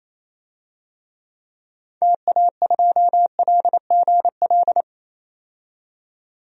Morse code, heard as a keyed tone, about 28 wpm, 700 Hz. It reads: TA2LGL
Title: DXCC-100 Call Signs - 6 Characters - 28wpm